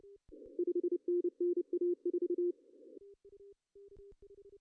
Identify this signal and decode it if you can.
Morse code, heard as a keyed tone, about 29 wpm, 350 Hz. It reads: 5NN A4
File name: CW / cqww-cw-k5tr